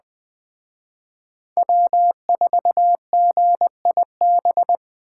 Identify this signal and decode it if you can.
Morse code, heard as a keyed tone, about 20 wpm, 700 Hz. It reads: W4GIB
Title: Speed Racing - USA 1x3 Call Signs 17wpm